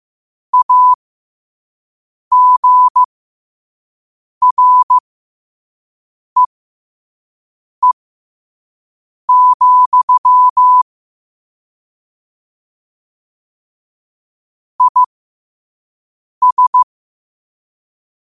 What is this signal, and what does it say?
Morse code, heard as a keyed tone, about 15 wpm, spaced as for 5 wpm, 1000 Hz. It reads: AGREE, IS
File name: qotdfiles / 5wpm